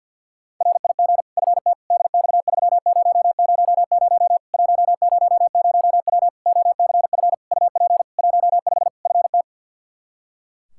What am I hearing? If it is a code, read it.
Morse code, heard as a keyed tone, about 52 wpm, 700 Hz. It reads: RIG FT DX3000 100W OCF UP 15 FT
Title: K1USN SST QSO Practice 50wpm